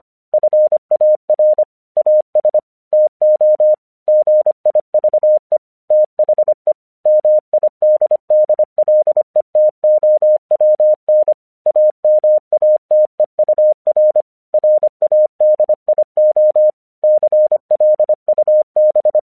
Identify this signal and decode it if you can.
Morse code, heard as a keyed tone, about 25 wpm, 610 Hz. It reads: FAR AS TO GIVE THE MIDDLETOWN AMATEUR RADIO CLUB
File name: 1 English 25wpm / AQC07